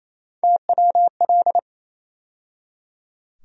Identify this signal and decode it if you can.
Morse code, heard as a keyed tone, about 28 wpm, 700 Hz. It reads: TWL
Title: USA Call Sign Postfixes - 3 Letters 28wpm